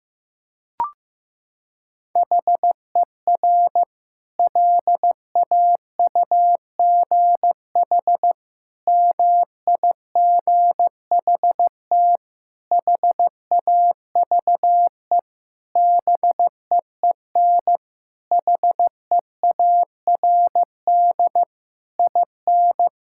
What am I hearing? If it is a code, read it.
Morse code, heard as a keyed tone, about 16 wpm, 700 Hz. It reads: HER LAUGH MIGHT HAVE BEEN HEARD IN
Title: Sentences from Top 1000 Words - Review Part 5 - 15wpm